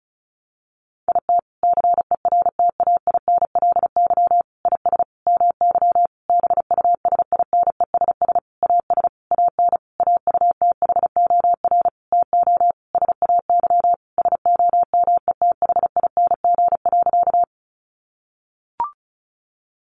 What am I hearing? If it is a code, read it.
Morse code, heard as a keyed tone, about 34 wpm, 700 Hz. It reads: IT CERTAINLY IS MY BUSINESS AS AN AUTHOR TO SAY SOMETHING.
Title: Sentences from Top 1200 Words 35wpm